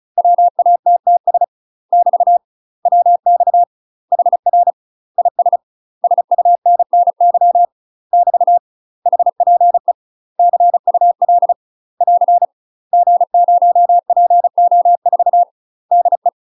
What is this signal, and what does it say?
Morse code, heard as a keyed tone, about 36 wpm, 700 Hz. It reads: WATTS <BT> WX HR IS SUNNY <BT> HPE CUL <AR> G0PO4 DE
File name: UK MASTER QSOs TRAINING / 35WPM / UK MASTER QSOs 0026